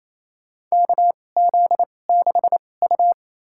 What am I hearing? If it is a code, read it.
Morse code, heard as a keyed tone, about 28 wpm, spaced as for 20 wpm, 700 Hz. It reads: KZ6U
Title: USA Call Signs - 4 Characters - 2x Character Spacing - 28wpm